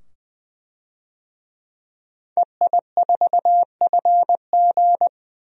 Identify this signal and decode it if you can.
Morse code, heard as a keyed tone, about 20 wpm, 700 Hz. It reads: EI4FG
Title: DXCC-100 Call Signs - 5 Characters - 20wpm